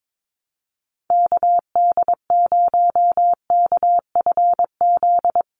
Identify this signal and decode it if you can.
Morse code, heard as a keyed tone, about 22 wpm, 700 Hz. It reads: KD0KFZ
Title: USA Call Signs 22wpm